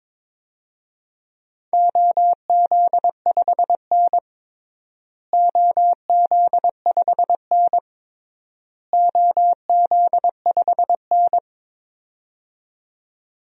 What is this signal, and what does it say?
Morse code, heard as a keyed tone, about 22 wpm, 700 Hz. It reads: OZ5N OZ5N OZ5N
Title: DXCC-100 Call Signs - 4 Characters - Repeated 3x - 22wpm